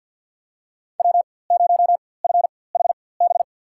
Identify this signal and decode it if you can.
Morse code, heard as a keyed tone, about 52 wpm, spaced as for 23 wpm, 700 Hz. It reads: W0FHB
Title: USA Call Signs - 5 Characters - 4x Character Spacing - 50wpm